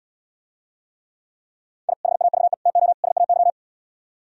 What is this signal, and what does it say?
Morse code, heard as a keyed tone, about 77 wpm, 700 Hz. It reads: I HAVE TO SAY